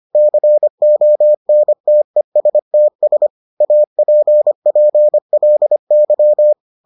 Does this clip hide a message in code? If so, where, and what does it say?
Yes: Morse code from 0.2 to 6.5 s, heard as a keyed tone, about 25 wpm, 600 Hz. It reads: CONTESTS APPLY